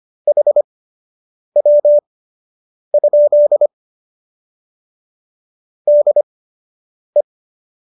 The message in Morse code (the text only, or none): HW? DE